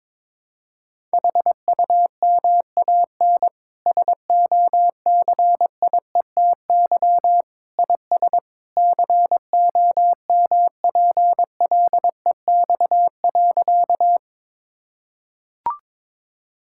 Morse code, heard as a keyed tone, about 22 wpm, 700 Hz. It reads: HUMAN SOCIETY IS COMPLEX.
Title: Sentences from Top 1000 Words - Review Part 6 - 22wpm